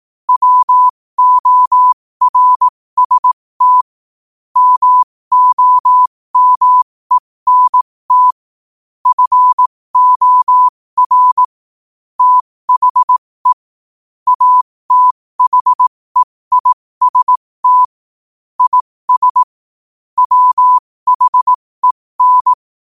Morse code, heard as a keyed tone, about 18 wpm, spaced as for 15 wpm, 1000 Hz. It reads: WORST MOMENT FOR THE ATHEIST IS WHEN
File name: qotdfiles / 15wpm